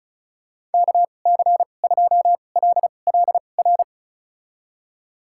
Morse code, heard as a keyed tone, about 34 wpm, spaced as for 25 wpm, 700 Hz. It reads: KC2LLR
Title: USA Call Signs - 6 Characters - 2x Character Spacing - 35wpm